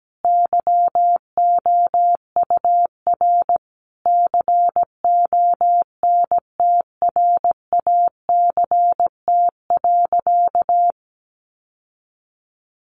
Morse code, heard as a keyed tone, about 17 wpm, 700 Hz. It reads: YOUR CONTRACT.
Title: Sentences from Top 1400 Words 17wpm